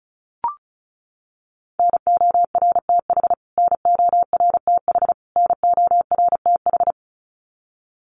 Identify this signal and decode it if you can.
Morse code, heard as a keyed tone, about 34 wpm, 700 Hz. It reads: NORTH NORTH NORTH
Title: Top 300 Words Repeated 35wpm